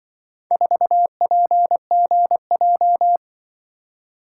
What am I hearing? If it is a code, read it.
Morse code, heard as a keyed tone, about 24 wpm, 700 Hz. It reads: 4PGJ
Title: Speed-Racing - Four Letter-Number Combinations - 20wpm